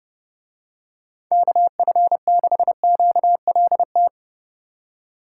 Morse code, heard as a keyed tone, about 30 wpm, 700 Hz. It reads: KF6QLT